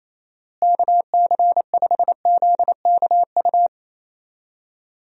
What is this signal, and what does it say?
Morse code, heard as a keyed tone, about 27 wpm, 700 Hz. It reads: KC5ZKU
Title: USA Call Signs - 6 Characters - Repeated 3x - 28wpm